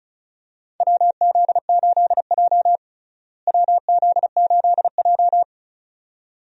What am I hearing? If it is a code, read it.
Morse code, heard as a keyed tone, about 36 wpm, 700 Hz. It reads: WZ8J WZ8J
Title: USA Call Signs - 4 Characters - Repeated 2x - 35wpm